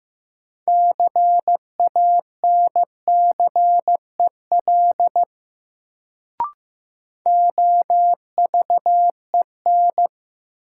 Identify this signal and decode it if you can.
Morse code, heard as a keyed tone, about 15 wpm, 700 Hz. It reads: CANCEL OVEN
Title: Sentences from Top 1200 Words - New Word Drill Alt - 15wpm